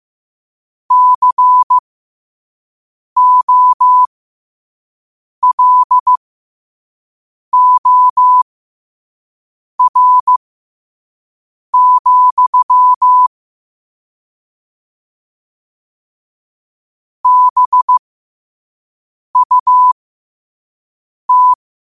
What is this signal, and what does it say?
Morse code, heard as a keyed tone, about 15 wpm, spaced as for 5 wpm, 1000 Hz. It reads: COLOR, BUT
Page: New quotes every day in morse code at 5 Words per minute.